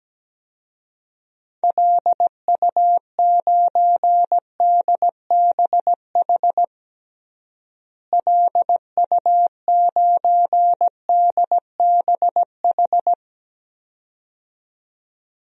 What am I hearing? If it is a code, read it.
Morse code, heard as a keyed tone, about 17 wpm, 700 Hz. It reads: LU9DBH LU9DBH
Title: DXCC-100 Call Signs - 6 Characters - Repeated 2x - 17wpm